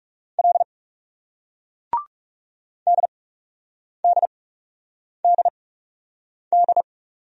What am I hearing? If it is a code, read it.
Morse code, heard as a keyed tone, about 38 wpm, spaced as for 8 wpm, 700 Hz. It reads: L DDDD
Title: Speed-Racing Single Letter-Number 30wpm